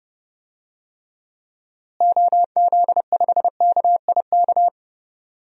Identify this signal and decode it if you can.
Morse code, heard as a keyed tone, about 31 wpm, 700 Hz. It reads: OZ5KIK
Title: DXCC-100 Call Signs - 6 Characters - Repeated 2x - 30wpm